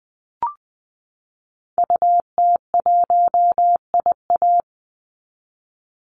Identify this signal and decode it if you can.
Morse code, heard as a keyed tone, about 20 wpm, 700 Hz. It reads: UT1IA
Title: Call Signs Repeated 3x - 20wpm